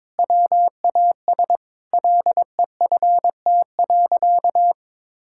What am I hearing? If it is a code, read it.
Morse code, heard as a keyed tone, about 22 wpm, 700 Hz. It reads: WAS LEFT.
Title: Sentences from Top 1600 Words 22wpm